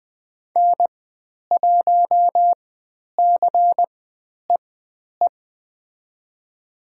Morse code, heard as a keyed tone, about 20 wpm, spaced as for 10 wpm, 700 Hz. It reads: N1CEE
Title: 102 Lesson 17g - Letter u - Call Signs 20wpm